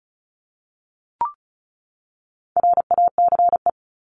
Morse code, heard as a keyed tone, about 34 wpm, 700 Hz. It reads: RACE
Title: Top 1501-1750 Words 35wpm